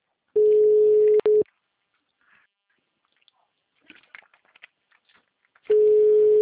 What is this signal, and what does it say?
Ringback tone (Europe, 425 Hz).